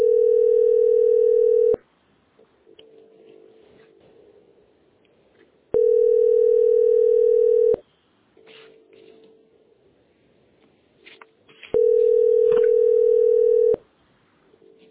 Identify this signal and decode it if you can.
Ringback tone (North America), 440 + 480 Hz.